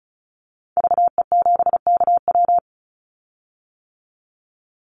Morse code, heard as a keyed tone, about 34 wpm, 700 Hz. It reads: VE7KW
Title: Call Signs - 35wpm